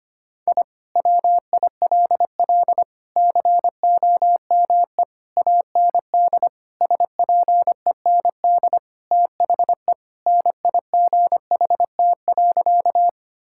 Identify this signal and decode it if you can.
Morse code, heard as a keyed tone, about 25 wpm, 700 Hz. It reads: I WILL COME AND SPEND THE NIGHT.
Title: Sentences from Top 1000 Words - Review Part 5 - 25wpm